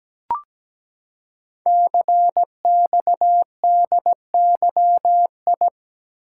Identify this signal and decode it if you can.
Morse code, heard as a keyed tone, about 17 wpm, 700 Hz. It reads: CXDYI
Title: Code Groups - 5 Letters - 17wpm